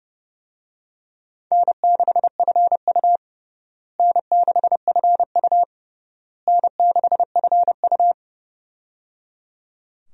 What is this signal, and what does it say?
Morse code, heard as a keyed tone, about 30 wpm, 700 Hz. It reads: N6FU N6FU N6FU
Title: USA Call Signs - 4 Characters - Repeated 3x - 30wpm